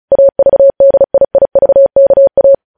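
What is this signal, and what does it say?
Morse code, heard as a keyed tone, about 36 wpm, 550 Hz. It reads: AVDIIVKA
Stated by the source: News headlines sent in Morse at 35 WPM.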